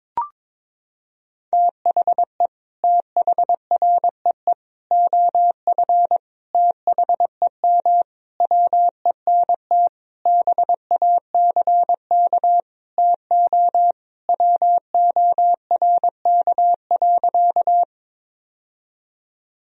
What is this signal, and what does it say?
Morse code, heard as a keyed tone, about 22 wpm, 700 Hz. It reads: THE THREE OF THEM WENT BACK TO WORK.
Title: Sentences from Top 1000 Words - Review Part 1 - 22wpm